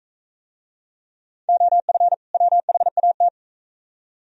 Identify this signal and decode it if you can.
Morse code, heard as a keyed tone, about 40 wpm, 700 Hz. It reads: OF WHAT